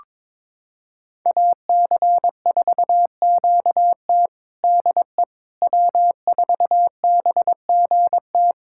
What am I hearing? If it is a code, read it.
Morse code, heard as a keyed tone, about 22 wpm, 700 Hz. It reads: AC4QT DE W4BGT
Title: USA Callsign DE Callsign - 5 Characters - 22wpm